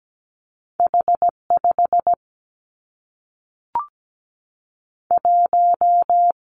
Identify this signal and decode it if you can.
Morse code, heard as a keyed tone, about 17 wpm, 700 Hz. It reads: H 5 1
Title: Two Challenging Letter-Number Combinations 17wpm